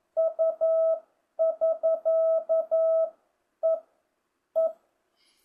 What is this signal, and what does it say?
Morse code, heard as a keyed tone, about 11 wpm, 650 Hz. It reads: U <SK> E E